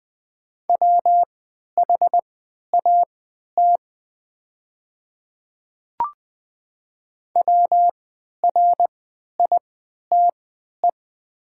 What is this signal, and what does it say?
Morse code, heard as a keyed tone, about 20 wpm, spaced as for 11 wpm, 700 Hz. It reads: WHAT WRITE
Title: 3x Character Spacing - Top 100 Words - 20wpm